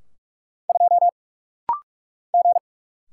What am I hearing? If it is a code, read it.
Morse code, heard as a keyed tone, about 44 wpm, 700 Hz. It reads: 2 G
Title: Single Letter-Number - Rapid-Fire 45wpm